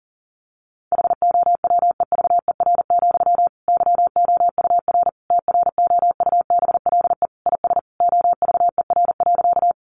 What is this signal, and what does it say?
Morse code, heard as a keyed tone, about 40 wpm, 700 Hz. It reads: HOWEVER, YOUR TROUBLE IS OVER.
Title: Sentences from Top 1200 Words 40wpm